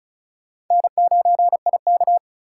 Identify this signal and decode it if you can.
Morse code, heard as a keyed tone, about 47 wpm, spaced as for 41 wpm, 700 Hz. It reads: N9IK
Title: Speed Racing - USA Call Signs 35wpm